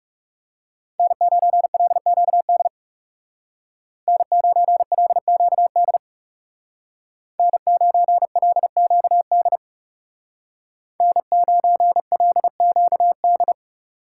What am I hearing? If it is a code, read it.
Morse code, heard as a keyed tone, about 37 wpm, 700 Hz. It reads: N9LQD N9LQD N9LQD N9LQD
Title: Speed Racing - USA 1x3 Call Signs 30wpm